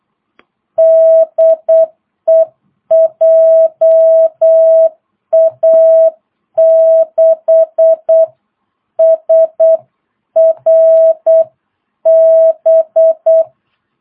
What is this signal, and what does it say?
Morse code, heard as a keyed tone, about 8 wpm, 650 Hz. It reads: DE JA6SRB
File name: jh8cbh / 2022cw